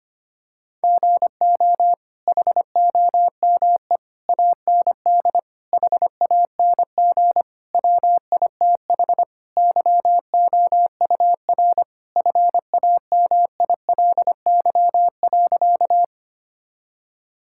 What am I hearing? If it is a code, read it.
Morse code, heard as a keyed tone, about 25 wpm, 700 Hz. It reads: GO HOME AND HANG WITH YOUR FAMILY.
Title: Sentences from Top 1600 Words 25wpm